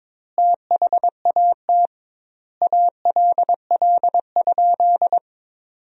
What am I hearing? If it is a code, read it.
Morse code, heard as a keyed tone, about 22 wpm, 700 Hz. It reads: THAT ALL?
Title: 2x Word-Spacing - Sentences from Top 200 Words - 22wpm